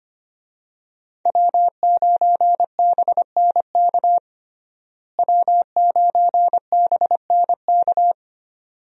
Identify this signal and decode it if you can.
Morse code, heard as a keyed tone, about 25 wpm, 700 Hz. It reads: W9BNK W9BNK
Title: USA Call Signs - 5 Characters - Repeated 2x - 25wpm